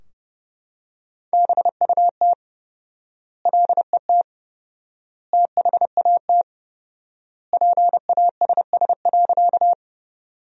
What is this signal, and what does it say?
Morse code, heard as a keyed tone, about 32 wpm, 700 Hz. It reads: BUT LET THAT PASS.
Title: 4x Word-Spacing - Sentences from Top 400 Words - 30wpm